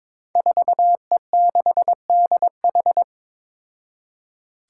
Morse code, heard as a keyed tone, about 21 wpm, 700 Hz. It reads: 4E6DH
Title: Five Challenging Letter-Number Combinations 22wpm